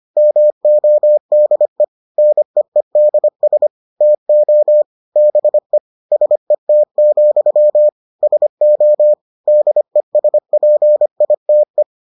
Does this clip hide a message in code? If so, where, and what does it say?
Yes: Morse code from 0.2 to 11.8 s, heard as a keyed tone, about 25 wpm, 600 Hz. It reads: MODE NEEDS TO BE SET, SO DESPITE